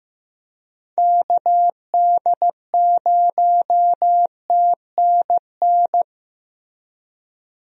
Morse code, heard as a keyed tone, about 15 wpm, 700 Hz. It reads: KD0TNN
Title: USA Call Signs - 6 Characters - Repeated 3x - 15wpm